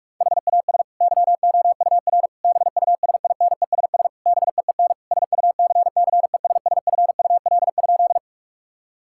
Morse code, heard as a keyed tone, about 46 wpm, 700 Hz. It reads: HAS YOUR BUSINESS BEEN SUCCESSFUL?
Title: Sentences from Top 1000 Words - Review Part 6 - 45wpm